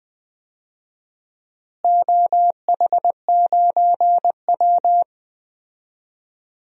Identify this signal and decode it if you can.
Morse code, heard as a keyed tone, about 20 wpm, 700 Hz. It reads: OH9W
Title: DXCC-100 Call Signs - 4 Characters - Repeated 3x - 20wpm